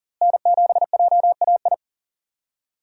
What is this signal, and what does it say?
Morse code, heard as a keyed tone, about 40 wpm, 700 Hz. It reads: N7JAI
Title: USA Call Signs - 5 Characters - Repeated 3x - 40wpm